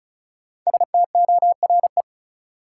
Morse code, heard as a keyed tone, about 34 wpm, 700 Hz. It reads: STORE
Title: Sentences from Top 1000 Words - Review Part 5 - New Word Drill - 35wpm